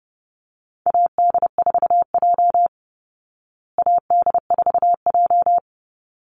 Morse code, heard as a keyed tone, about 30 wpm, 700 Hz. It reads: AD4J AD4J
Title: Call Signs Repeated 3x - 30wpm